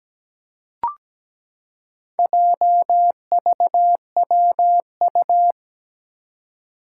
Morse code, heard as a keyed tone, about 17 wpm, 700 Hz. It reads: JVWU
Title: Four Challenging Letter-Number Combinations 17wpm